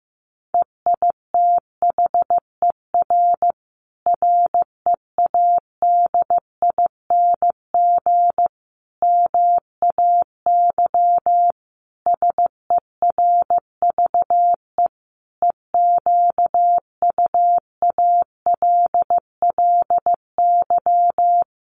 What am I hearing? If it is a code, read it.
Morse code, heard as a keyed tone, about 15 wpm, 700 Hz. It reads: EITHER READING MAY SERVE EQUALLY
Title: Sentences from Top 1400 Words 15wpm